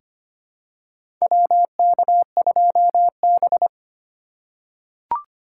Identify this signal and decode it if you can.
Morse code, heard as a keyed tone, about 25 wpm, 700 Hz. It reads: WK2B
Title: USA Call Signs - 4 Characters - 4x Character Spacing - 25wpm